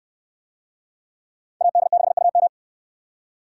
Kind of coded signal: Morse code

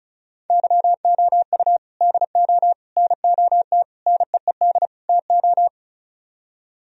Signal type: Morse code

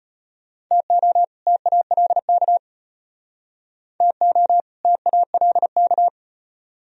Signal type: Morse code

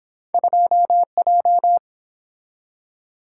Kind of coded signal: Morse code

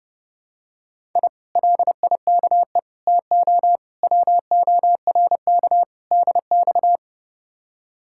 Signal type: Morse code

